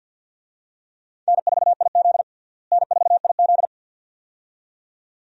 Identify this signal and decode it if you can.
Morse code, heard as a keyed tone, about 46 wpm, 700 Hz. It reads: N4IZ N4IZ
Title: USA Call Signs - 4 Characters - Repeated 2x - 50wpm